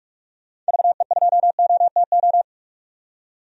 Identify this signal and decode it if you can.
Morse code, heard as a keyed tone, about 45 wpm, 700 Hz. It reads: VE2OTO
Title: DXCC-100 Call Signs - 6 Characters - 45wpm